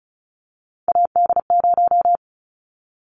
Morse code, heard as a keyed tone, about 36 wpm, 690 Hz. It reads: AD0
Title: Callsign Prefixes for Top 25 DXCC - 35wpm